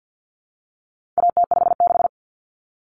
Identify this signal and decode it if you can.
Morse code, heard as a keyed tone, about 50 wpm, 700 Hz. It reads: AT56
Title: Common Cut Numbers - 4-Digit 50wpm